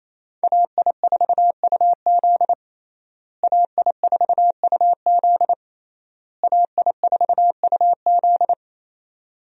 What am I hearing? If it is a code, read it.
Morse code, heard as a keyed tone, about 27 wpm, 700 Hz. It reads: AI4UZ AI4UZ AI4UZ
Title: USA Call Signs - 5 Characters - Repeated 3x - 28wpm